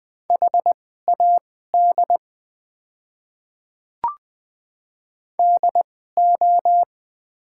Morse code, heard as a keyed tone, about 20 wpm, spaced as for 14 wpm, 700 Hz. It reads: HAD DO
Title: 2x Character Spacing - Top 100 Words - 20wpm